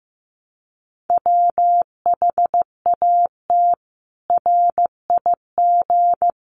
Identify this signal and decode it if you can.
Morse code, heard as a keyed tone, about 15 wpm, 700 Hz. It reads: WHAT RIG
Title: Common QSO Phrases 15wpm